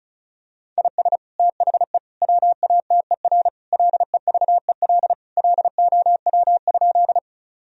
Morse code, heard as a keyed tone, about 36 wpm, 700 Hz. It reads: IS THE WATER LEVEL LOW?
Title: Sentences from Top 1000 Words - Review Part 4 - 35wpm